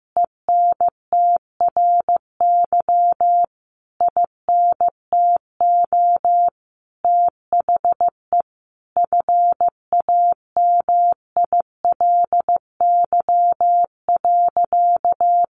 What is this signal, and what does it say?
Morse code, heard as a keyed tone, about 15 wpm, 700 Hz. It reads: ENTRY INTO THE FAMILY.
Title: Sentences from Top 1500 Words 15wpm